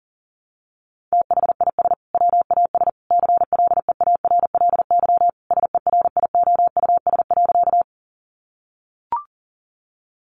Morse code, heard as a keyed tone, about 40 wpm, 700 Hz. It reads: THIS WAS CLEARLY SERIOUS.
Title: Sentences from Top 1300 Words 40wpm